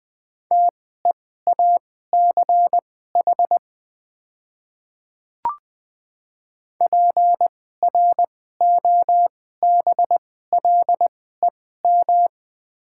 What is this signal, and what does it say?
Morse code, heard as a keyed tone, about 20 wpm, spaced as for 14 wpm, 700 Hz. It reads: TEACH PROBLEM
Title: 2x Character Spacing - Top 500 Words - 20wpm